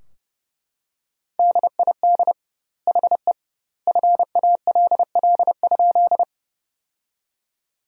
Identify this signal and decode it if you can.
Morse code, heard as a keyed tone, about 30 wpm, 700 Hz. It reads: DID HE FALL?
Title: 2x Word-Spacing - Sentences from Top 500 Words - 30wpm